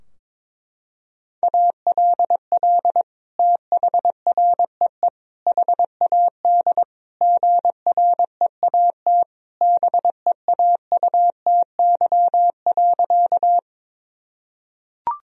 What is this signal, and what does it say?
Morse code, heard as a keyed tone, about 22 wpm, 700 Hz. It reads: ALL THREE HAD GREAT BEAUTY.
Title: Sentences from Top 1000 Words - Review Part 2 - 22wpm